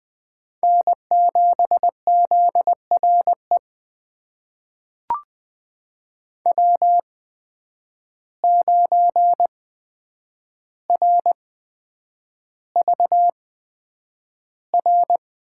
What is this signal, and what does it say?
Morse code, heard as a keyed tone, about 20 wpm, spaced as for 9 wpm, 700 Hz. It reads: N7ZRE W9RVR
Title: USA Call Signs - 5 Characters - 8x Character Spacing - 20wpm